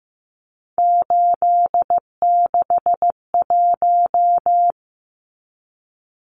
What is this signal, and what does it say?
Morse code, heard as a keyed tone, about 15 wpm, 700 Hz. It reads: 861
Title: Numbers - 3-Digit 15wpm